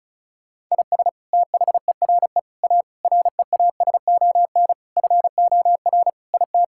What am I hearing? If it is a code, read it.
Morse code, heard as a keyed tone, about 34 wpm, 700 Hz. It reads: IS THERE A REASON FOR IT
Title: Sentences from Top 1000 Words - Review Part 5 - 35wpm